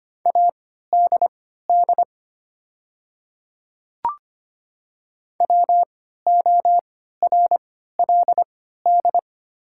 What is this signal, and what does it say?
Morse code, heard as a keyed tone, about 25 wpm, spaced as for 14 wpm, 700 Hz. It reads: ADD WORLD
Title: 3x Character Spacing - Top 200 Words - 25wpm